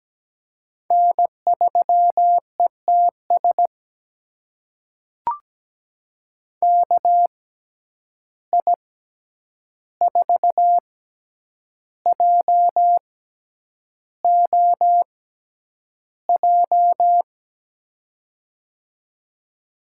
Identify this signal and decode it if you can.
Morse code, heard as a keyed tone, about 17 wpm, spaced as for 8 wpm, 700 Hz. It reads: N3ETS KI4JOJ
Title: USA Call Signs - 6x Character Spacing - 17wpm